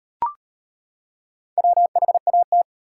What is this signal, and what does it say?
Morse code, heard as a keyed tone, about 37 wpm, 700 Hz. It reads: WHAT